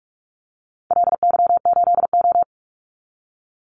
Morse code, heard as a keyed tone, about 44 wpm, 700 Hz. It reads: LY8O
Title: Call Signs Repeated 2x - 45wpm